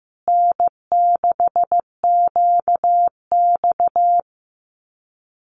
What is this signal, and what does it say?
Morse code, heard as a keyed tone, about 15 wpm, 700 Hz. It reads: N6QX
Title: Common QSO Phrases Encore 15wpm